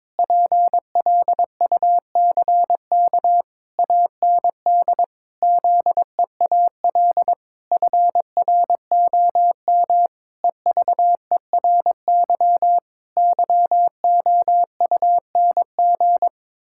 Morse code, heard as a keyed tone, about 22 wpm, 700 Hz. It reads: PLUCK AND ZEAL FROM EVERY YOUNG
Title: Pangram Challenge v2 - 22wpm